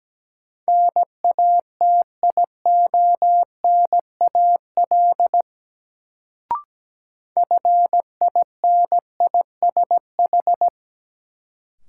Morse code, heard as a keyed tone, about 17 wpm, 700 Hz. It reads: NATIONAL FINISH
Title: Sentences from Top 1200 Words - New Word Drill Alt - 17wpm